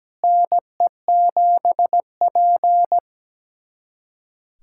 Morse code, heard as a keyed tone, about 17 wpm, 700 Hz. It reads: NE7P
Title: USA Callsign DE Callsign - 4 Characters - 17wpm